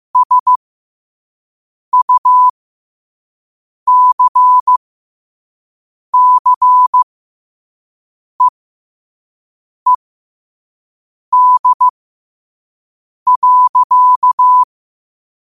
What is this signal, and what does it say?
Morse code, heard as a keyed tone, about 15 wpm, spaced as for 5 wpm, 1000 Hz. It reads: SUCCEED.
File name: 5wpm